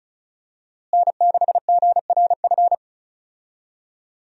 Morse code, heard as a keyed tone, about 36 wpm, 700 Hz. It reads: N6GRF
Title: USA Call Signs - 5 Characters - Repeated 3x - 35wpm